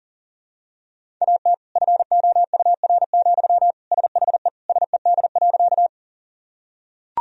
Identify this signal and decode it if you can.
Morse code, heard as a keyed tone, about 40 wpm, 700 Hz. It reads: AT FOUR, SHE SED.
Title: Sentences from Top 300 Words and Abbreviations 40wpm